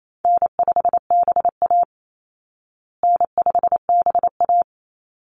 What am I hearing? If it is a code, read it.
Morse code, heard as a keyed tone, about 28 wpm, 700 Hz. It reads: N5BA N5BA
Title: Easy Call Signs Repeated 3x - 28wpm